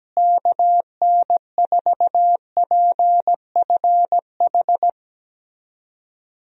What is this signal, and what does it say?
Morse code, heard as a keyed tone, about 17 wpm, 700 Hz. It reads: KN4PFH
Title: USA Call Signs - 6 Characters - Repeated 3x - 17wpm